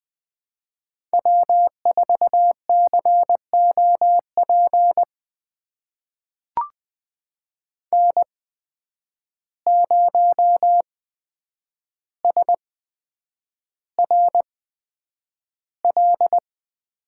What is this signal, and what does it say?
Morse code, heard as a keyed tone, about 20 wpm, spaced as for 9 wpm, 700 Hz. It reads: W4COP N0SRL
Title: USA Call Signs - 5 Characters - 8x Character Spacing - 20wpm